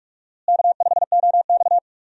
Morse code, heard as a keyed tone, about 46 wpm, 700 Hz. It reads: K5OX
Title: USA Call Signs - 4 Characters - Repeated 3x - 45wpm